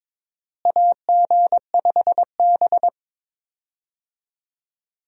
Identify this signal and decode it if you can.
Morse code, heard as a keyed tone, about 22 wpm, 700 Hz. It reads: AG5B
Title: USA Call Signs - 4 Characters - Repeated 2x - 22wpm